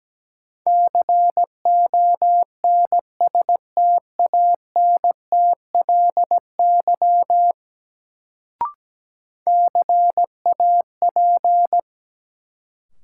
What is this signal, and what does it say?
Morse code, heard as a keyed tone, about 17 wpm, 700 Hz. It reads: CONSTANTLY CAP
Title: Sentences from Top 1600 Words - New Word Drill Alt - 17wpm